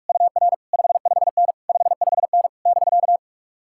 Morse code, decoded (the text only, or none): UR 55N 55N <BK>